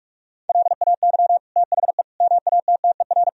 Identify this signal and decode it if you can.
Morse code, heard as a keyed tone, about 46 wpm, 700 Hz. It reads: LAY THE MATTER
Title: Sentences from Top 1000 Words - Review Part 4 - 45wpm